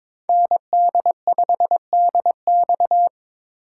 Morse code, heard as a keyed tone, about 26 wpm, 700 Hz. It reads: ND5DX
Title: Speed Racing - USA 2x2 Call Signs 22wpm